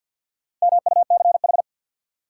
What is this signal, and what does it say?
Morse code, heard as a keyed tone, about 53 wpm, 700 Hz. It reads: MUCH
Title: Sets of 2 Words - New Word Drill - 50wpm